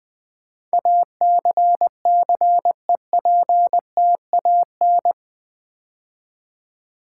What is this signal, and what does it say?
Morse code, heard as a keyed tone, about 20 wpm, 700 Hz. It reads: ACCEPTAN
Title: Build A Word - 4 Letters At A Time - 10-Letter Words - 20wpm